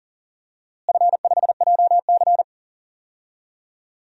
Morse code, heard as a keyed tone, about 40 wpm, 700 Hz. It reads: F5JC
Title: DXCC-100 Call Signs - 4 Characters - Repeated 3x - 40wpm